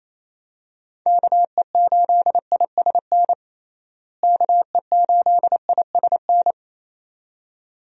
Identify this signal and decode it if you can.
Morse code, heard as a keyed tone, about 28 wpm, 700 Hz. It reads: KE8ISN KE8ISN
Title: USA Call Signs - 6 Characters - Repeated 2x - 28wpm